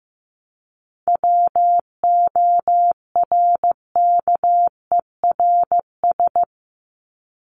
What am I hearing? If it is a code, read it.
Morse code, heard as a keyed tone, about 15 wpm, 700 Hz. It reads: WORKERS
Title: ES:S Suffix - Top 100 Words with Base Words 15wpm